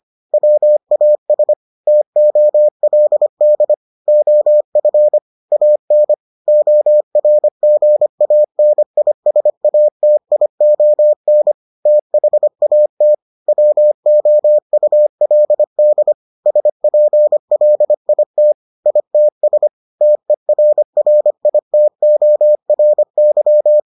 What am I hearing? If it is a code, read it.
Morse code, heard as a keyed tone, about 25 wpm, 600 Hz. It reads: WAS TOLD OF AN ORGANISATION THAT WOULD SPLIT ITS TERRITORY